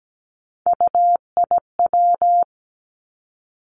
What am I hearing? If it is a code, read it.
Morse code, heard as a keyed tone, about 17 wpm, 700 Hz. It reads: UIW
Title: Three Challenging Letter-Number Combinations 17wpm